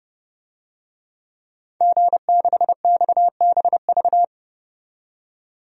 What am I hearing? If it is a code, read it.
Morse code, heard as a keyed tone, about 32 wpm, 700 Hz. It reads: G6XBV
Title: DXCC-100 Call Signs - 5 Characters - 30wpm